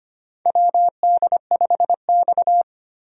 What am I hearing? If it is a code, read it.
Morse code, heard as a keyed tone, about 25 wpm, 700 Hz. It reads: WD5X
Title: USA Call Signs - 4 Characters - 25wpm